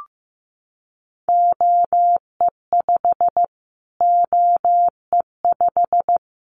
Morse code, heard as a keyed tone, about 15 wpm, 700 Hz. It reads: OE5 OE5
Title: Callsign Prefixes for Top 25 DXCC - Repeated 3x - 15wpm